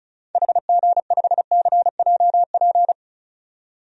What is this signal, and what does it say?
Morse code, heard as a keyed tone, about 34 wpm, 700 Hz. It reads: HG5CJP
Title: DXCC-100 Call Signs - 6 Characters - 35wpm